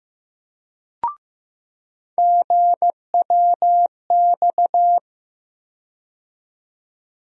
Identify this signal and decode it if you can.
Morse code, heard as a keyed tone, about 15 wpm, 700 Hz. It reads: GWX
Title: USA Call Sign Postfixes - 3 Letters 15wpm